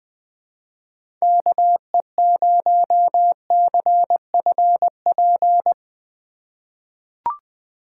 Morse code, heard as a keyed tone, about 20 wpm, 700 Hz. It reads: KE0CFP
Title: USA Call Signs - 4x Character Spacing - 20wpm